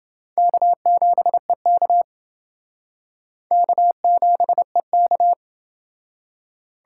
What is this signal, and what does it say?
Morse code, heard as a keyed tone, about 28 wpm, 700 Hz. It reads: K7EK K7EK